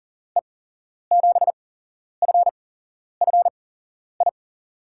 Morse code, heard as a keyed tone, about 38 wpm, spaced as for 11 wpm, 700 Hz. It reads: E7FFI
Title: USA Call Signs - 6 Characters - 8x Character Spacing - 40wpm